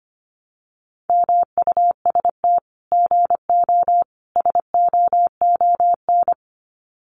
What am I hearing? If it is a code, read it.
Morse code, heard as a keyed tone, about 25 wpm, 700 Hz. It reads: MUST GO SOON
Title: Common QSO Phrases 25wpm